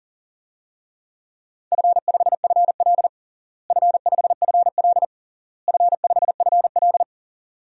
Morse code, heard as a keyed tone, about 40 wpm, 700 Hz. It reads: F5FL F5FL F5FL
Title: DXCC-100 Call Signs - 4 Characters - Repeated 3x - 40wpm